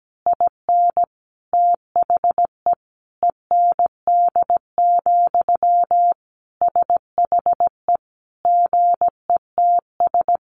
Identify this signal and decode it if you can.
Morse code, heard as a keyed tone, about 17 wpm, 700 Hz. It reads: IN THE END, SHE GETS
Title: Additional Easy Phrases 17wpm